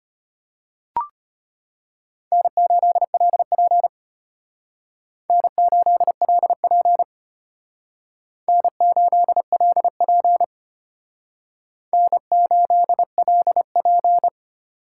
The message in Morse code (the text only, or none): N8LP N8LP N8LP N8LP